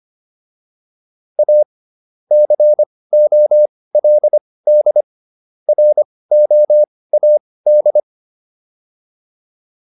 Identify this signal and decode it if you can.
Morse code, heard as a keyed tone, about 25 wpm, spaced as for 18 wpm, 600 Hz. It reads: A COLD ROAD